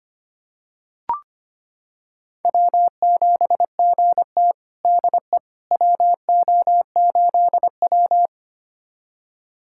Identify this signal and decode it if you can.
Morse code, heard as a keyed tone, about 25 wpm, 700 Hz. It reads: W7GT DE WO8W
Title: USA Callsign DE Callsign - 4 Characters - 25wpm